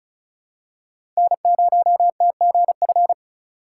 Morse code, heard as a keyed tone, about 34 wpm, 700 Hz. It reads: N0TGF
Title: USA Call Signs - 4x Character Spacing - 35wpm